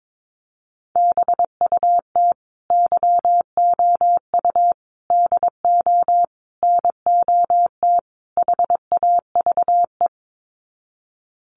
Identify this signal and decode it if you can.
Morse code, heard as a keyed tone, about 22 wpm, 700 Hz. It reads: BUT YOU DO NOT HAVE
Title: Sets of 5 Words Encore 22wpm